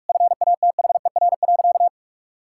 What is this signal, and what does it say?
Morse code, heard as a keyed tone, about 41 wpm, 700 Hz. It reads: FATHER.
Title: Sentences from Top 1000 Words - Review Part 5 - 45wpm